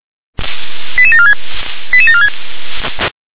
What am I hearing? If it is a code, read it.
Five-tone selective call (ZVEI), heard as quick runs of five tones, tones of about 80 ms. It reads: 88746 88746